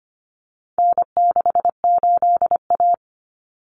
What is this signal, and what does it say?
Morse code, heard as a keyed tone, about 25 wpm, 700 Hz. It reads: N68A
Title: Common Cut Numbers - 4-Digit 25wpm